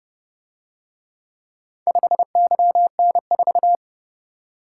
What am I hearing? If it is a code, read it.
Morse code, heard as a keyed tone, about 30 wpm, 700 Hz. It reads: 5YN4